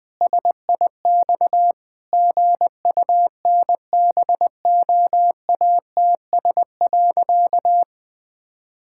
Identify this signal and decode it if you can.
Morse code, heard as a keyed tone, about 20 wpm, 700 Hz. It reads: SIX GUNBOATS.
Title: Pangram Challenge v2 - 20wpm